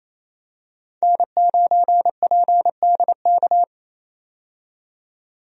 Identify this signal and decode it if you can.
Morse code, heard as a keyed tone, about 28 wpm, 700 Hz. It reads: N9PDK
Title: USA Call Signs - 5 Characters - 28wpm